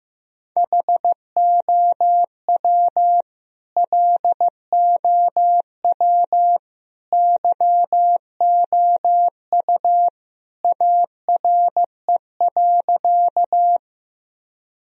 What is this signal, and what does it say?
Morse code, heard as a keyed tone, about 15 wpm, 700 Hz. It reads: HOW LOW YOU ARE.
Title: Sentences from Top 1000 Words - Review Part 1 - 15wpm